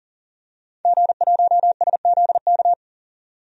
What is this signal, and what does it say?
Morse code, heard as a keyed tone, about 40 wpm, 700 Hz. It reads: G1SZK
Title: DXCC-100 Call Signs - 5 Characters - 40wpm